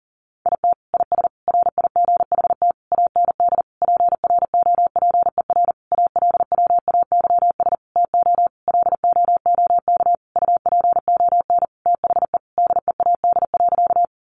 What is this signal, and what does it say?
Morse code, heard as a keyed tone, about 40 wpm, 700 Hz. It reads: IT IS RIGHT AND PROPER ALWAYS TO LOOK UPON THE DEAD.
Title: Sentences from Top 1500 Words 40wpm